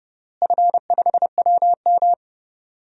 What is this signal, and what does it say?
Morse code, heard as a keyed tone, about 30 wpm, 700 Hz. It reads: F5WM
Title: DXCC-100 Call Signs - 4 Characters - Repeated 3x - 30wpm